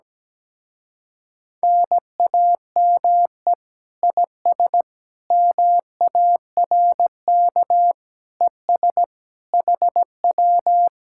Morse code, heard as a keyed tone, about 17 wpm, 700 Hz. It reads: NAME IS MARK ES HW
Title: Speed-Racing Common QSO Phrases - 17wpm